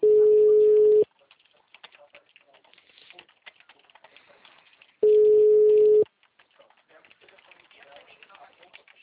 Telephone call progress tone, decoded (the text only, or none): ringback tone